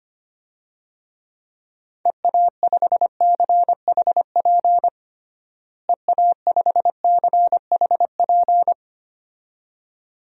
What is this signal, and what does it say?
Morse code, heard as a keyed tone, about 25 wpm, 700 Hz. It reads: EA5CHP EA5CHP
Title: DXCC-100 Call Signs - 6 Characters - Repeated 2x - 25wpm